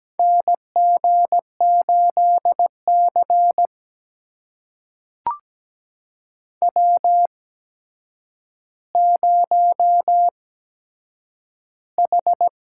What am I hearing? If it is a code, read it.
Morse code, heard as a keyed tone, about 17 wpm, spaced as for 8 wpm, 700 Hz. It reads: NG8C W0H
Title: USA Call Signs - 4 Characters - 8x Character Spacing - 17wpm